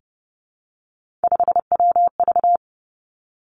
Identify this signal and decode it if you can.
Morse code, heard as a keyed tone, about 30 wpm, 700 Hz. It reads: 5WV
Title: Three Challenging Letter-Number Combinations 30wpm